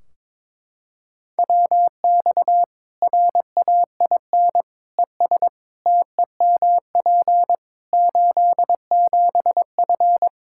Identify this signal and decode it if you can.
Morse code, heard as a keyed tone, about 21 wpm, 700 Hz. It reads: WX RAIN ES TEMP 87F
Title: Practice QSO - Exchange 2 - Info - 22wpm